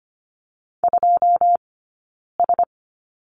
Morse code, heard as a keyed tone, about 25 wpm, 700 Hz. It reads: 2 S
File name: course / 25:10wpm / 092 Lesson 16d - Letter c - Two Characters 25wpm